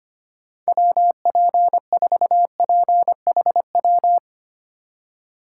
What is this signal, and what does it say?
Morse code, heard as a keyed tone, about 25 wpm, 700 Hz. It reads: WP4PHW
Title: USA Call Signs - 4x Character Spacing - 25wpm